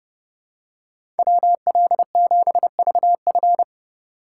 Morse code, heard as a keyed tone, about 30 wpm, 700 Hz. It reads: WL7VF
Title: USA Call Signs - 5 Characters - 2x Character Spacing - 30wpm